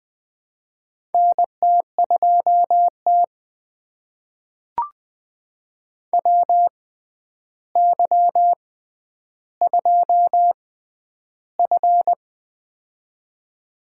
Morse code, heard as a keyed tone, about 20 wpm, spaced as for 10 wpm, 700 Hz. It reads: NT2T WY2F
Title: USA Call Signs - 4 Characters - 6x Character Spacing - 20wpm